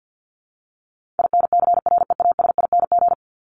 Morse code, heard as a keyed tone, about 50 wpm, 700 Hz. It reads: INCREASING
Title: ING Suffix - Top 100 Words with Base Words 50wpm